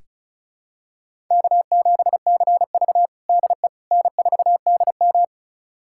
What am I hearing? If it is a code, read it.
Morse code, heard as a keyed tone, about 36 wpm, 700 Hz. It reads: K7CV DE N4DM
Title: USA Callsign DE Callsign - 4 Characters - 35wpm